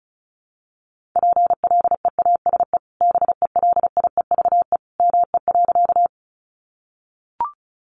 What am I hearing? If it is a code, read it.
Morse code, heard as a keyed tone, about 34 wpm, 700 Hz. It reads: PLEASE BELIEVE ME.
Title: Sentences from Top 1500 Words 35wpm